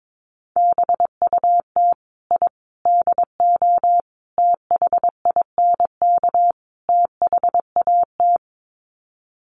Morse code, heard as a keyed tone, about 22 wpm, 700 Hz. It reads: BUT I DO THINK THAT
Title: Sets of 5 Words Encore2 22wpm